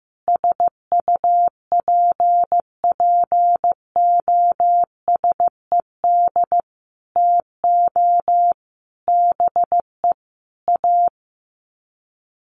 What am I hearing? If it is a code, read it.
Morse code, heard as a keyed tone, about 15 wpm, 700 Hz. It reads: SUPPOSED TO BE A
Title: Sets of 5 Words Encore 15wpm